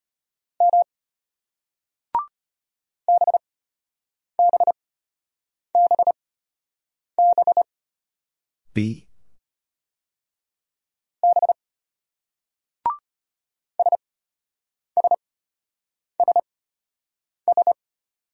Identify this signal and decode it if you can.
Morse code, heard as a keyed tone, about 33 wpm, spaced as for 8 wpm, 700 Hz. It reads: M BBBB B SSSS